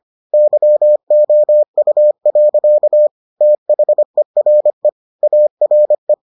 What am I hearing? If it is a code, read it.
Morse code, heard as a keyed tone, about 25 wpm, 600 Hz. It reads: YOU. THERE ARE